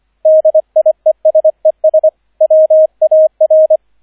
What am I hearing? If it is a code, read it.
Morse code, heard as a keyed tone, about 25 wpm, 630 Hz, with radio noise. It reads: DIESES WAR